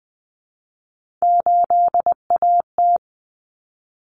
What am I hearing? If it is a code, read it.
Morse code, heard as a keyed tone, about 20 wpm, 700 Hz. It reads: 8AT
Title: Common Cut Numbers - 3-Digit 20wpm